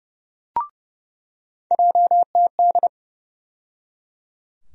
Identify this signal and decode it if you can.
Morse code, heard as a keyed tone, about 30 wpm, 700 Hz. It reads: JTD
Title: USA Call Sign Postfixes - 3 Letters 30wpm